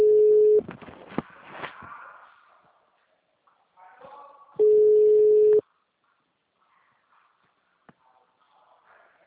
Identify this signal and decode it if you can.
Ringback tone (Europe, 425 Hz).
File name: out- -2002- - - .12